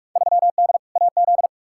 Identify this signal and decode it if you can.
Morse code, heard as a keyed tone, about 44 wpm, 700 Hz. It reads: 3D AZ
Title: Field Day Exchanges - 45wpm